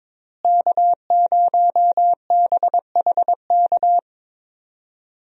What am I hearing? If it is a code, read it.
Morse code, heard as a keyed tone, about 22 wpm, 700 Hz. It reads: K0BHK
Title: USA Call Signs - 5 Characters - Repeated 3x - 22wpm